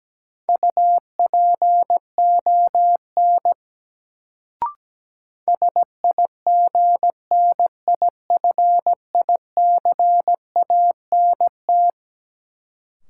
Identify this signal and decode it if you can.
Morse code, heard as a keyed tone, about 17 wpm, 700 Hz. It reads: UPON SIGNIFICANT
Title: Sentences from Top 1000 Words - Review Part 6 - New Word Drill - 17wpm